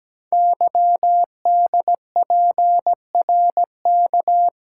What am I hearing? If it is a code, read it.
Morse code, heard as a keyed tone, about 17 wpm, 700 Hz. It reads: YDPRK
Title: Code Groups - 5 Letters - 17wpm